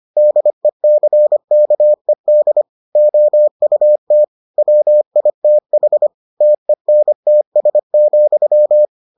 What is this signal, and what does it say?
Morse code, heard as a keyed tone, about 25 wpm, 600 Hz. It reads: DECKED OUT WITH TENTS,